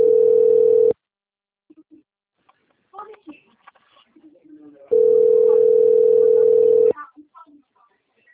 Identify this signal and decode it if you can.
Ringback tone (North America), 440 + 480 Hz, with speech.
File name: rg-1600-unknown- - - .32139